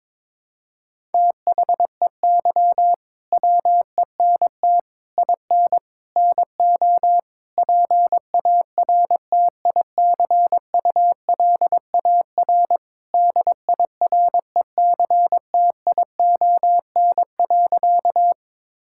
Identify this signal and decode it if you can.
Morse code, heard as a keyed tone, about 22 wpm, 700 Hz. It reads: THEY WENT IN NO PARTICULAR DIRECTION.
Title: Sentences from Top 1000 Words - Review Part 5 - 22wpm